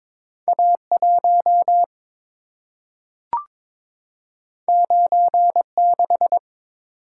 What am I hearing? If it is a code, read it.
Morse code, heard as a keyed tone, about 22 wpm, 690 Hz. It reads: A1 96
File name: practice / 22wpm / Two Letter-Number Combinations - 22wpm